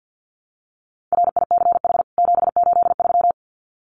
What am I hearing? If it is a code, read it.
Morse code, heard as a keyed tone, about 50 wpm, 700 Hz. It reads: RICH 783
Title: CWT Exchange 50wpm